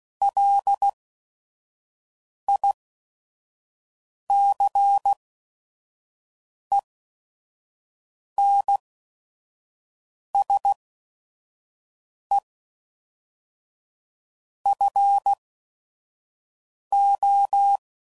Morse code, heard as a keyed tone, about 16 wpm, spaced as for 5 wpm, 800 Hz. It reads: LICENSE FO
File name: CW Code Groups @5WPM / QSOs / kc7haa